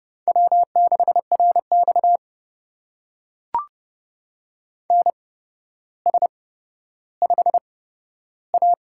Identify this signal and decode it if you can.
Morse code, heard as a keyed tone, about 31 wpm, spaced as for 13 wpm, 700 Hz. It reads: W6RX NS5A
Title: USA Call Signs - 4 Characters - 8x Character Spacing - 30wpm